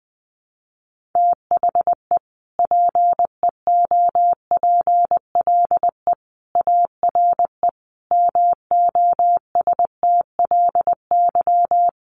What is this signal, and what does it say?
Morse code, heard as a keyed tone, about 20 wpm, 700 Hz. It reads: THE PEOPLE ARE MOSTLY
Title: Sentences from Top 1300 Words 20wpm